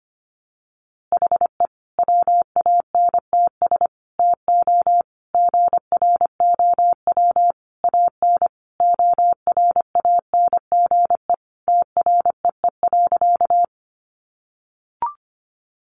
Morse code, heard as a keyed tone, about 25 wpm, 700 Hz. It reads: HE WANTS TO GROW AN ORANGE TREE.
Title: Easy Phrases 25wpm